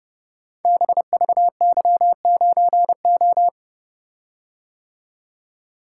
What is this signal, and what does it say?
Morse code, heard as a keyed tone, about 30 wpm, 700 Hz. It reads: BVY9O
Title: Code Groups - 5 Letter:Numbers - 30wpm